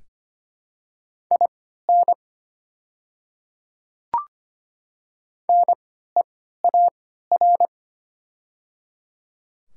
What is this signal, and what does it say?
Morse code, heard as a keyed tone, about 25 wpm, spaced as for 14 wpm, 700 Hz. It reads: IN NEAR
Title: 3x Character Spacing - Top 300 Words - 25wpm